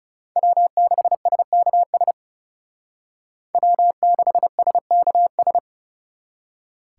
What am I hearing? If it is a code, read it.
Morse code, heard as a keyed tone, about 34 wpm, 700 Hz. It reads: W6SKS W6SKS
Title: Speed Racing - USA 1x3 Call Signs 30wpm